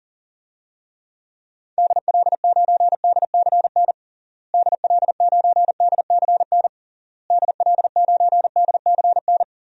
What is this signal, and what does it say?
Morse code, heard as a keyed tone, about 40 wpm, 700 Hz. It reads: DL9DCN DL9DCN DL9DCN
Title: DXCC-100 Call Signs - 6 Characters - Repeated 3x - 40wpm